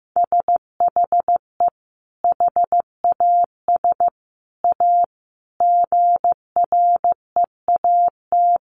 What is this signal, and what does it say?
Morse code, heard as a keyed tone, about 15 wpm, 700 Hz. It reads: SHE HAS A GREAT
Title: Sentences from Top 1100 Words 15wpm